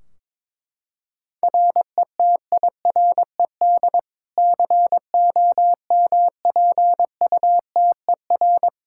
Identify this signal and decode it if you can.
Morse code, heard as a keyed tone, about 21 wpm, 700 Hz. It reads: RETIRED COMPUTER
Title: Practice QSO - Exchange 3 - More Info - 22wpm